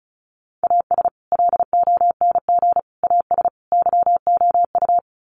Morse code, heard as a keyed tone, about 34 wpm, 700 Hz. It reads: AS LONG AS YOU
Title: Sets of 4 Words 35wpm